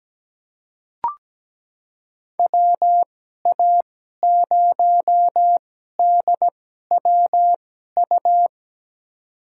Morse code, heard as a keyed tone, about 17 wpm, spaced as for 12 wpm, 700 Hz. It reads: WA0DWU
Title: USA Call Signs - 6 Characters - 2x Character Spacing - 17wpm